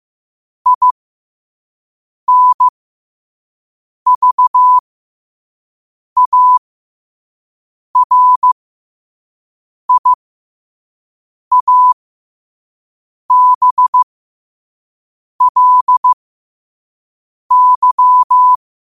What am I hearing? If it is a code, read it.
Morse code, heard as a keyed tone, about 15 wpm, spaced as for 5 wpm, 1000 Hz. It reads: INVARIABLY
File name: qotdfiles / 5wpm